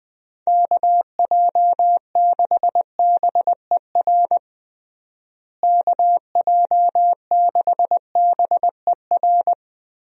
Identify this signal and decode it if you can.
Morse code, heard as a keyed tone, about 20 wpm, 700 Hz. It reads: KJ6BER KJ6BER
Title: USA Call Signs - 6 Characters - Repeated 3x - 20wpm